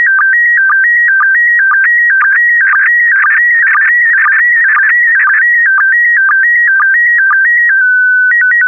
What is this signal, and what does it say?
SSTV (slow-scan television), PD 120.
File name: ISS-SSTV / SSTV-Audio-ISS-Bild-02